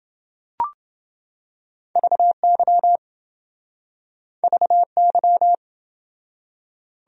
Morse code, heard as a keyed tone, about 27 wpm, 700 Hz. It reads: VY VY